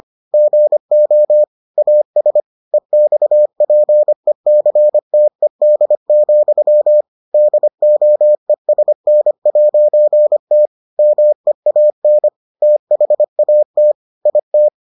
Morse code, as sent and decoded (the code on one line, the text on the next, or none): --. --- / .- ... / . -..- .--. -.-. - . -.. --..-- / -.. --- . ... -. .----. - / -- . .- -. / - .... .- - / .. -
GO AS EXPECTED, DOESN'T MEAN THAT IT